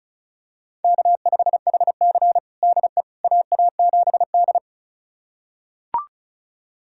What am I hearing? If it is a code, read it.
Morse code, heard as a keyed tone, about 36 wpm, 700 Hz. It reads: K5HC DE AA7D
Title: USA Callsign DE Callsign - 4 Characters - 35wpm